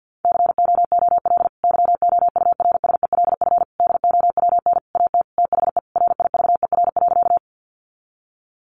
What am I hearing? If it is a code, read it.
Morse code, heard as a keyed tone, about 50 wpm, 700 Hz. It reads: COOL YOURSELF DOWN AT THE RIVER.
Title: Sentences from Top 1200 Words 50wpm